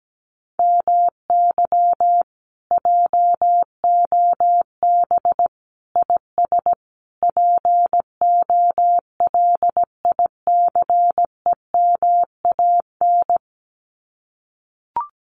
Morse code, heard as a keyed tone, about 17 wpm, 700 Hz. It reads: MY JOB IS POLICEMAN
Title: Common QSO Phrases Encore 17wpm